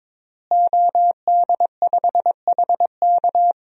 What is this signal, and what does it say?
Morse code, heard as a keyed tone, about 22 wpm, 700 Hz. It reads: OD5HK
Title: DXCC-100 Call Signs - 5 Characters - Repeated 3x - 22wpm